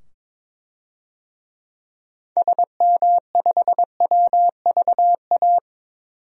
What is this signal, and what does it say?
Morse code, heard as a keyed tone, about 22 wpm, 700 Hz. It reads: SM5WVA
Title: DXCC-100 Call Signs - 6 Characters - 22wpm